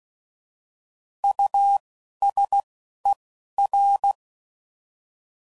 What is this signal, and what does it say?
Morse code, heard as a keyed tone, about 16 wpm, spaced as for 12 wpm, 800 Hz. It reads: USER
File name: Prefixes / By First Letter / Us